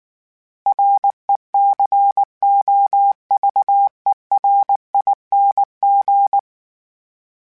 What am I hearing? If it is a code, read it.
Morse code, heard as a keyed tone, about 19 wpm, 800 Hz. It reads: RECOVERING
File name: Special INGs / Special INGs-mp3s / Sngl Ing @18